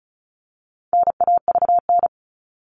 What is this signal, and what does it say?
Morse code, heard as a keyed tone, about 34 wpm, 700 Hz. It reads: NAVN
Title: All Cut Numbers - 4-Digit 35wpm